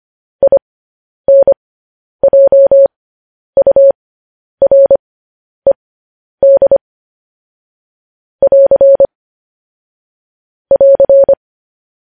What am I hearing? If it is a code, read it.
Morse code, heard as a keyed tone, about 25 wpm, spaced as for 10 wpm, 550 Hz. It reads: INJURED <AR> <AR>
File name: podcast / News headlines-1x-25x10